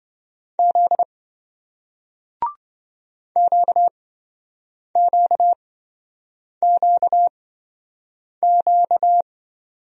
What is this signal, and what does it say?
Morse code, heard as a keyed tone, about 28 wpm, spaced as for 7 wpm, 700 Hz. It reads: Z QQQQ